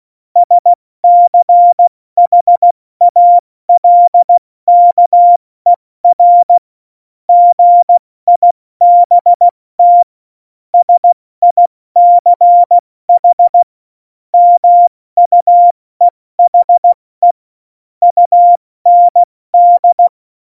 Morse code, heard as a keyed tone, about 16 wpm, spaced as for 14 wpm, 700 Hz. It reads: SCHALKER GIBT SICH MUEHE UND